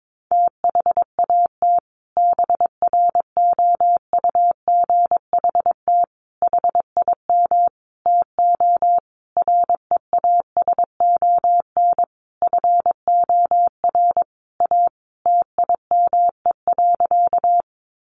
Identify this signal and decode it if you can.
Morse code, heard as a keyed tone, about 22 wpm, 700 Hz. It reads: THAT BROUGHT HIM TO REASON FOR A TIME.
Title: Sentences from Top 700 Words 22wpm